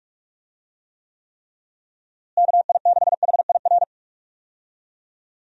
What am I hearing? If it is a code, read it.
Morse code, heard as a keyed tone, about 46 wpm, 700 Hz. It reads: KI6HIR
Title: USA Call Signs - 6 Characters - Repeated 3x - 45wpm